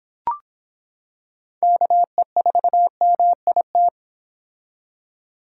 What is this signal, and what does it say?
Morse code, heard as a keyed tone, about 25 wpm, 700 Hz. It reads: KE4MIT